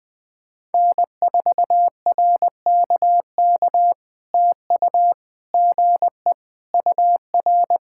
Morse code, heard as a keyed tone, about 20 wpm, 700 Hz. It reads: N4RKK TU GE UR
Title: POTA QSO Practice - 20wpm